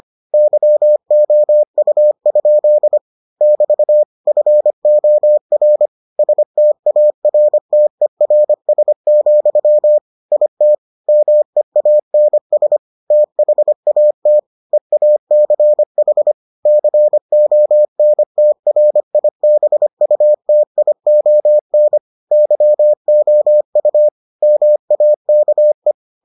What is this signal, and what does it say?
Morse code, heard as a keyed tone, about 25 wpm, 600 Hz. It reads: YOU? <BT> FOR STARTERS, IT MEANS THAT EACH CONTRIBUTION YOU MAKE